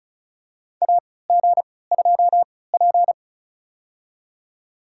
Morse code, heard as a keyed tone, about 36 wpm, spaced as for 20 wpm, 700 Hz. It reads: AG2P
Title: USA Call Signs - 4 Characters - 3x Character Spacing - 35wpm